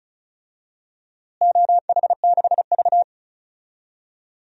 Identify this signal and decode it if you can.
Morse code, heard as a keyed tone, about 34 wpm, 700 Hz. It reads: OH6V
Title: DXCC-100 Call Signs - 4 Characters - Repeated 2x - 35wpm